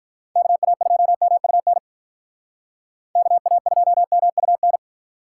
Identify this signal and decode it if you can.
Morse code, heard as a keyed tone, about 45 wpm, 700 Hz. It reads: KA2MUN KA2MUN